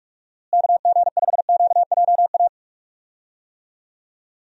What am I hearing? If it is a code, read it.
Morse code, heard as a keyed tone, about 46 wpm, 700 Hz. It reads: KG5QJA
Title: USA Call Signs - 6 Characters - Repeated 3x - 45wpm